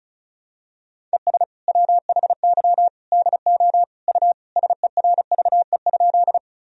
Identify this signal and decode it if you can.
Morse code, heard as a keyed tone, about 36 wpm, 700 Hz. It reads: ES WHY DO U SERVE?
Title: Sentences from Top 400 Words and Abbreviations 35wpm